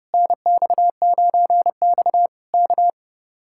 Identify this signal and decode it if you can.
Morse code, heard as a keyed tone, about 28 wpm, 700 Hz. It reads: NX9X K
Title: Speed-Racing Common QSO Phrases Encore - 30wpm